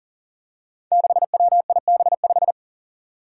Morse code, heard as a keyed tone, about 39 wpm, 700 Hz. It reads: 6WIB5
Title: Five Challenging Letter-Number Combinations 40wpm